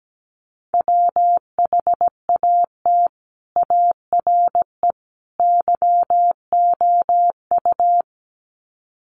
Morse code, heard as a keyed tone, about 17 wpm, 700 Hz. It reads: WHAT ARE YOU
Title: Sets of 3 Words 17wpm